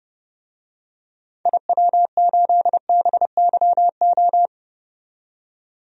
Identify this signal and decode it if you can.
Morse code, heard as a keyed tone, about 30 wpm, 700 Hz. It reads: IW8BYO
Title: DXCC-100 Call Signs - 6 Characters - 30wpm